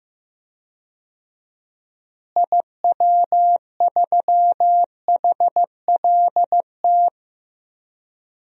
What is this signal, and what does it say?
Morse code, heard as a keyed tone, about 15 wpm, 700 Hz. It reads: IW3HLT
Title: DXCC-100 Call Signs - 6 Characters - Repeated 3x - 15wpm